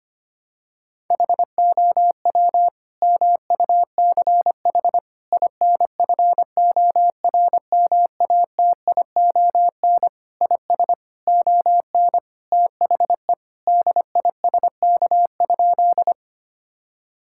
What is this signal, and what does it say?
Morse code, heard as a keyed tone, about 25 wpm, 700 Hz. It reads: HOW MUCH INFORMATION IS ON THE DISK?
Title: Sentences from Top 1000 Words - Review Part 6 - 25wpm